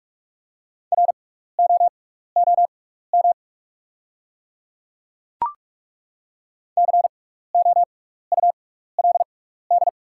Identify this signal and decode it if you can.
Morse code, heard as a keyed tone, about 43 wpm, spaced as for 15 wpm, 690 Hz. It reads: ROOM COULD
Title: 6x Character Spacing - Top 300 Words - 45wpm